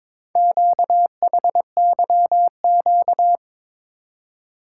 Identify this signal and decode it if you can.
Morse code, heard as a keyed tone, about 22 wpm, 700 Hz. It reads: QHYQ
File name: practice / 22wpm / Four Letter-Number Combinations - 22wpm